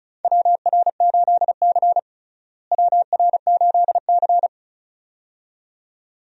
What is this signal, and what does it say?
Morse code, heard as a keyed tone, about 36 wpm, 700 Hz. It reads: WR8C WR8C
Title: USA Call Signs - 4 Characters - Repeated 2x - 35wpm